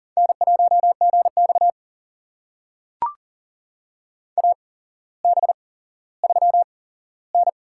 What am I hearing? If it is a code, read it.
Morse code, heard as a keyed tone, about 38 wpm, spaced as for 17 wpm, 700 Hz. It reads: N1GX AB3N
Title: USA Call Signs - 4 Characters - 8x Character Spacing - 40wpm